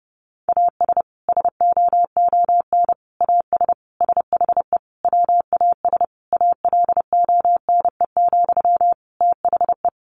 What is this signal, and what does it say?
Morse code, heard as a keyed tone, about 30 wpm, 700 Hz. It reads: AS SOON AS SHE WAS ALONE, THE
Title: Sentences from Top 1100 Words 30wpm